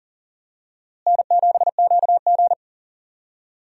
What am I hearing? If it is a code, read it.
Morse code, heard as a keyed tone, about 40 wpm, 700 Hz. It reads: N7QG
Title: USA Call Signs - 4 Characters - Repeated 2x - 40wpm